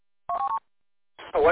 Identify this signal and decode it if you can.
Speech. There is no code.